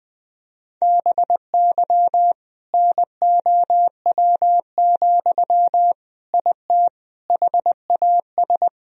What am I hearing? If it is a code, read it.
Morse code, heard as a keyed tone, about 20 wpm, 700 Hz. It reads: BY NOW, IT HAS
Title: Sentences from Top 1000 Words - Review Part 1 - 20wpm